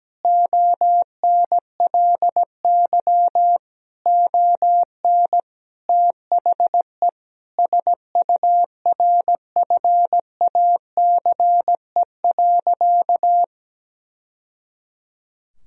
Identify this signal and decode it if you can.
Morse code, heard as a keyed tone, about 17 wpm, 700 Hz. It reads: ONLY ON THE SURFACE.
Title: Sentences from Top 1000 Words - Review Part 3 - 17wpm